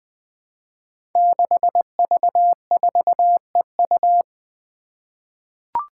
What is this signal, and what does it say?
Morse code, heard as a keyed tone, about 20 wpm, 700 Hz. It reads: 6V4EU
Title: Five Challenging Letter-Number Combinations 20wpm